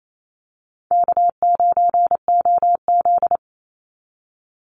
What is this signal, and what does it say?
Morse code, heard as a keyed tone, about 28 wpm, 700 Hz. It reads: K9OZ
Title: Call Signs Repeated 3x - 28wpm